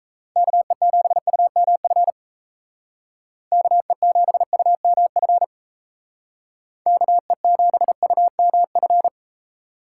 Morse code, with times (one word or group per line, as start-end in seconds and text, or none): 0.4-2.1 KE7UMF
3.5-5.4 KE7UMF
6.9-9.1 KE7UMF